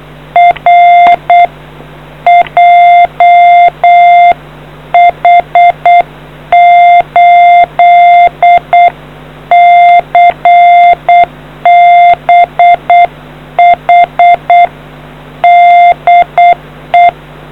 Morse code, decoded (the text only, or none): R JH8CBH DE